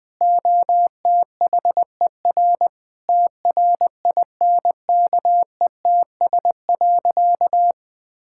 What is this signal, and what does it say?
Morse code, heard as a keyed tone, about 21 wpm, 700 Hz. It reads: OTHER TRINKETS.
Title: Sentences with 2-Word Phrases - 20wpm